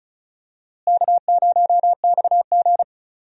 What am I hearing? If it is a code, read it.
Morse code, heard as a keyed tone, about 36 wpm, 700 Hz. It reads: K0XG
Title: USA Call Signs - 4 Characters - Repeated 3x - 35wpm